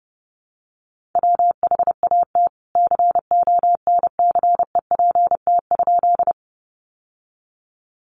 Morse code, heard as a keyed tone, about 30 wpm, 700 Hz. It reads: WHAT CONCEPT?
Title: Sentences from Top 1100 Words 30wpm